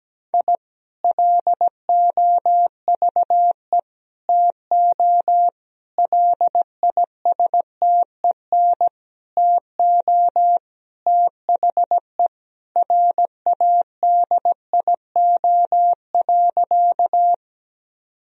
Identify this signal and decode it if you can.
Morse code, heard as a keyed tone, about 17 wpm, 700 Hz. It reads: I LOVE TO LISTEN TO THE RADIO.
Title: Sentences from Top 1000 Words - Review Part 3 - 17wpm